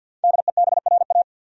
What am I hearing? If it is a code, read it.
Morse code, heard as a keyed tone, about 52 wpm, 700 Hz. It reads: DEBRA
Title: USA Common Names - 50wpm